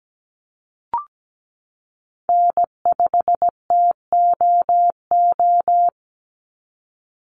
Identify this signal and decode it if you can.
Morse code, heard as a keyed tone, about 17 wpm, 700 Hz. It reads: N5TOO
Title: Call Signs Repeated 3x - 17wpm